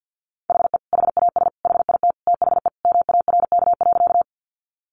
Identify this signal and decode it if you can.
Morse code, heard as a keyed tone, about 50 wpm, 700 Hz. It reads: HE HAS HIT THE MARK.
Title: Sentences from Top 1200 Words 50wpm